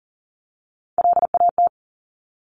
Morse code, heard as a keyed tone, about 41 wpm, 700 Hz. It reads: LAT
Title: Top Three Letter Combinations in Words 40wpm